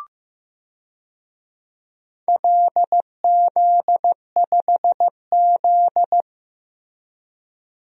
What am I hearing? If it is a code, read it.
Morse code, heard as a keyed tone, about 16 wpm, 700 Hz. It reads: LZ5Z
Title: DXCC-100 Call Signs - 4 Characters - Repeated 3x - 15wpm